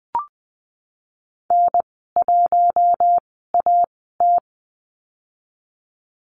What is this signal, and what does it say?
Morse code, heard as a keyed tone, about 20 wpm, spaced as for 15 wpm, 700 Hz. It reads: N1AT
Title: 042 Lesson 8g - Number 1 - Call Signs 20wpm